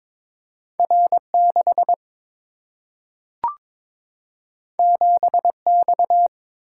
Morse code, heard as a keyed tone, about 22 wpm, 700 Hz. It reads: R6 7X
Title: Single-Letter Single-Number 22wpm